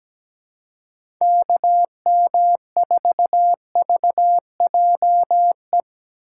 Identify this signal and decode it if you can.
Morse code, heard as a keyed tone, about 17 wpm, 700 Hz. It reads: KM4VJE
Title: USA Call Signs - 6 Characters - 6x Character Spacing - 17wpm